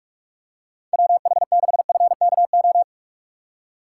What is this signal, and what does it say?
Morse code, heard as a keyed tone, about 45 wpm, 700 Hz. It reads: WH6FKO
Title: USA Call Signs - 6 Characters - 3x Character Spacing - 45wpm